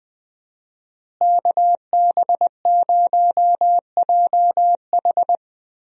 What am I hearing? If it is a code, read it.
Morse code, heard as a keyed tone, about 20 wpm, 700 Hz. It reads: KB0JH
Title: USA Call Signs - 5 Characters - Repeated 3x - 20wpm